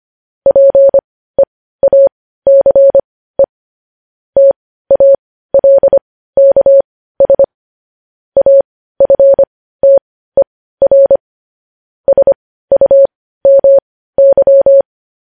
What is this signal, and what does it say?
Morse code, heard as a keyed tone, about 25 wpm, spaced as for 15 wpm, 550 Hz. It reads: PEACE TALKS AFTER SUMY
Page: News Headlines in Morse Code at 15 WPM